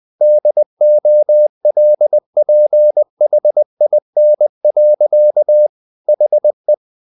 Morse code, heard as a keyed tone, about 20 wpm, 600 Hz. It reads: DOLPHIN. HE
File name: the old man and the sea part 2 20wpm